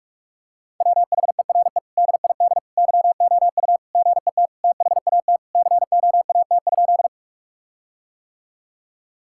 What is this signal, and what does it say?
Morse code, heard as a keyed tone, about 41 wpm, 700 Hz. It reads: WHERE DID YOU GET THAT COAT?
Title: Sentences from Top 1600 Words 45wpm